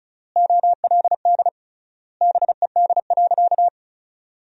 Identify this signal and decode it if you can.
Morse code, heard as a keyed tone, about 34 wpm, 700 Hz. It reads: OLD BED.
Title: 3x Word-Spacing - Sentences from Top 500 Words - 35wpm